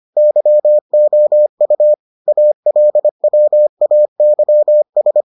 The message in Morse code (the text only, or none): YOU ALWAYS